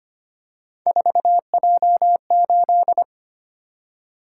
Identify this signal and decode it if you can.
Morse code, heard as a keyed tone, about 25 wpm, 700 Hz. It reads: 4J8
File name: Three Letter-Number Combinations - 25wpm